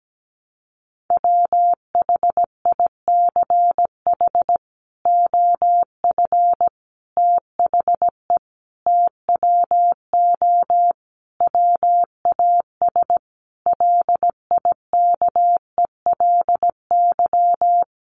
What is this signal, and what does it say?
Morse code, heard as a keyed tone, about 17 wpm, 700 Hz. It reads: WHICH OF THE TWO WAS LIKELY
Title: Sentences from Top 1300 Words 17wpm